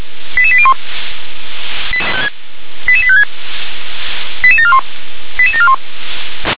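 Five-tone selective call (ZVEI), heard as quick runs of five tones, tones of about 70 ms. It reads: 88801 88856 88741 88741
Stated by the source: ZVEI five-tone selective call received on an SDR.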